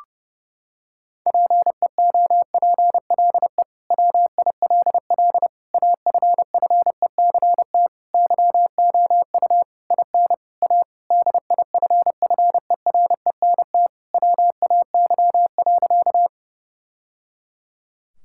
Morse code, heard as a keyed tone, about 30 wpm, 700 Hz. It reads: PEOPLE WILL AFFECT YOU IN A DIFFERENT WAY.
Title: Sentences from Top 1000 Words - Review Part 6 - 30wpm